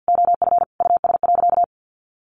Morse code, heard as a keyed tone, about 50 wpm, 700 Hz. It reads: OF US.
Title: Sentences from Top 1300 Words 50wpm